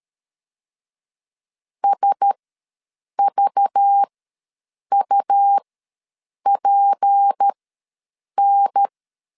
Morse code, heard as a keyed tone, about 13 wpm, spaced as for 7 wpm, 800 Hz. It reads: SVUPN
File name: 7 GF9Z ws 12wpm